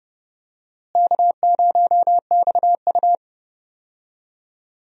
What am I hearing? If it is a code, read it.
Morse code, heard as a keyed tone, about 30 wpm, 700 Hz. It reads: K0XU
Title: USA Call Signs - 4 Characters - Repeated 2x - 30wpm